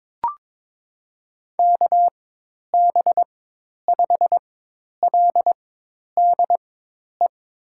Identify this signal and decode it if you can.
Morse code, heard as a keyed tone, about 21 wpm, spaced as for 10 wpm, 700 Hz. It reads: KB5LDE
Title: USA Call Signs - 6 Characters - 4x Character Spacing - 22wpm